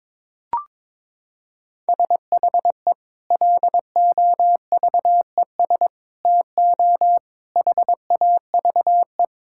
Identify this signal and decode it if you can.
Morse code, heard as a keyed tone, about 22 wpm, 700 Hz. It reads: SHE LOVES TO HAVE
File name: practice / 22wpm / Sentences with 2-Word Phrases - 22wpm